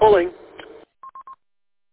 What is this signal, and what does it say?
Speech. There is no code.